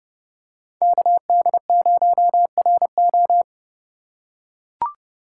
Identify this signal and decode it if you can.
Morse code, heard as a keyed tone, about 29 wpm, 700 Hz. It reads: KD0RO
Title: USA Call Signs - 3x Character Spacing - 30wpm